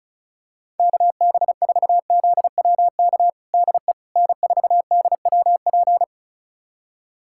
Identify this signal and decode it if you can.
Morse code, heard as a keyed tone, about 34 wpm, 700 Hz. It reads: KB4ZWK DE N4DWP
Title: USA Callsign DE Callsign 35wpm